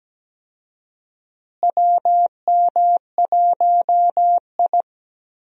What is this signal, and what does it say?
Morse code, heard as a keyed tone, about 17 wpm, 700 Hz. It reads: WM1I
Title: USA Call Signs - 4 Characters - Repeated 3x - 17wpm